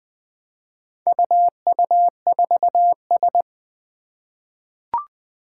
Morse code, heard as a keyed tone, about 20 wpm, 700 Hz. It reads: UU4S
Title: Four Challenging Letter-Number Combinations 20wpm